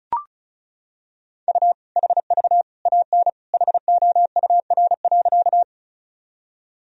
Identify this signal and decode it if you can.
Morse code, heard as a keyed tone, about 36 wpm, 700 Hz. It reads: U HV AN HOUR.
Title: Sentences from Top 400 Words and Abbreviations 35wpm